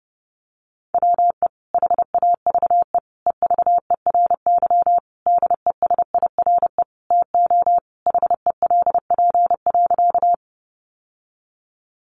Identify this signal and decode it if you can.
Morse code, heard as a keyed tone, about 30 wpm, 700 Hz. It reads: WE HAVE EVERY DESIRE TO HELP.
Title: Sentences from Top 1100 Words 30wpm